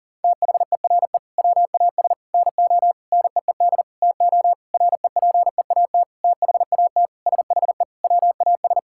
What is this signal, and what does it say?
Morse code, heard as a keyed tone, about 40 wpm, 700 Hz. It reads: THERE WAS NO NEED TO REPEAT THAT SHE WAS
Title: Sentences from Top 1600 Words 40wpm